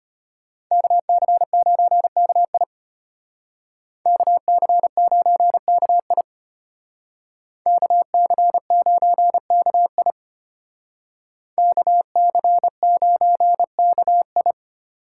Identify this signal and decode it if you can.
Morse code, heard as a keyed tone, about 33 wpm, 700 Hz. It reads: KC9KI KC9KI KC9KI KC9KI